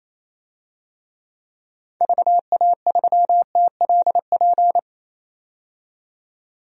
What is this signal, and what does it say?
Morse code, heard as a keyed tone, about 28 wpm, 700 Hz. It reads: VA3TLP
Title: DXCC-100 Call Signs - 6 Characters - 28wpm